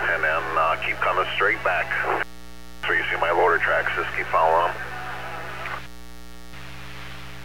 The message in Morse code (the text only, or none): none